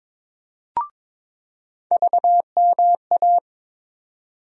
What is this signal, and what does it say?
Morse code, heard as a keyed tone, about 22 wpm, 700 Hz. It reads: VMA
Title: USA Call Sign Postfixes - 3 Letters 22wpm